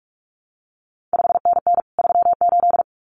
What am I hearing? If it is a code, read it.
Morse code, heard as a keyed tone, about 46 wpm, 700 Hz. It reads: 5NN 38
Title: CQ WW DX Contest Exchange Encore - 45wpm